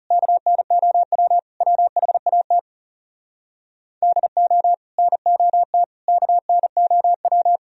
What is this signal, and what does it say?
Morse code, heard as a keyed tone, about 40 wpm, 700 Hz. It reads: KNOW WHAT DO NOT KNOW